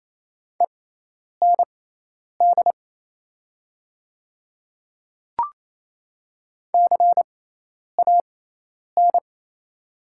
Morse code, heard as a keyed tone, about 27 wpm, spaced as for 10 wpm, 700 Hz. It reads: END CAN
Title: 6x Character Spacing - Top 300 Words - 28wpm